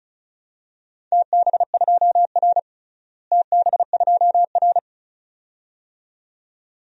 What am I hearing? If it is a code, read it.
Morse code, heard as a keyed tone, about 36 wpm, 700 Hz. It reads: TB2R TB2R
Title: DXCC-100 Call Signs - 4 Characters - Repeated 2x - 35wpm